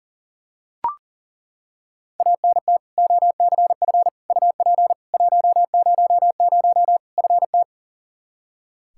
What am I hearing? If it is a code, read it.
Morse code, heard as a keyed tone, about 37 wpm, 700 Hz. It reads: ANT OCF UP 100 FT
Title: Practice QSO - Exchange 2 - Info - 40wpm